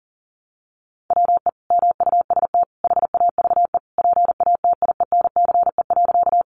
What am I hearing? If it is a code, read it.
Morse code, heard as a keyed tone, about 40 wpm, 700 Hz. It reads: WE MUST HAVE PATIENCE.
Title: Sentences from Top 1300 Words 40wpm